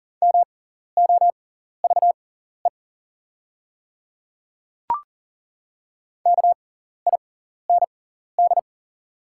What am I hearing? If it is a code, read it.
Morse code, heard as a keyed tone, about 38 wpm, spaced as for 14 wpm, 690 Hz. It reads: MOVE KIND
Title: 6x Character Spacing - Top 200 Words - 40wpm